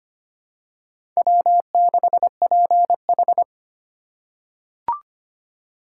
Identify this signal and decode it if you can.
Morse code, heard as a keyed tone, about 25 wpm, 700 Hz. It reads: W6PH
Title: USA Call Signs - 4 Characters - 3x Character Spacing - 25wpm